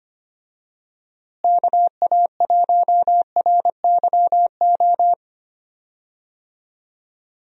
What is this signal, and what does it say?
Morse code, heard as a keyed tone, about 25 wpm, 700 Hz. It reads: KA1RYO
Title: USA Call Signs - 6 Characters - Repeated 3x - 25wpm